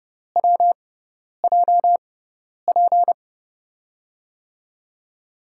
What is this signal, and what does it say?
Morse code, heard as a keyed tone, about 30 wpm, spaced as for 10 wpm, 700 Hz. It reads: WJP
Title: USA Call Signs - 6 Characters - 6x Character Spacing - 30wpm